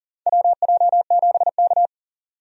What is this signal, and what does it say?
Morse code, heard as a keyed tone, about 37 wpm, 700 Hz. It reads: WJ7K
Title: USA Call Signs - 4 Characters - 40wpm